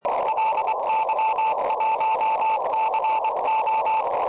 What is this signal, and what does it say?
Morse code, heard as a keyed tone, about 24 wpm, 880 Hz, with radio noise. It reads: LY1CO
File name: tropo / ly1co